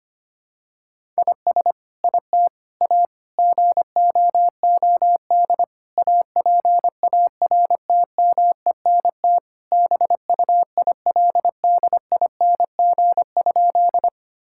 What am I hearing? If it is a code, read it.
Morse code, heard as a keyed tone, about 25 wpm, 700 Hz. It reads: IS IT A GOOD APARTMENT BUILDING?
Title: Sentences from Top 1000 Words - Review Part 6 - 25wpm